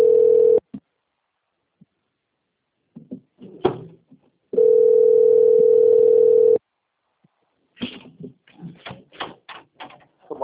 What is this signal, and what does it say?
Ringback tone (North America), 440 + 480 Hz.
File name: rg-1600-unknown- - - .18045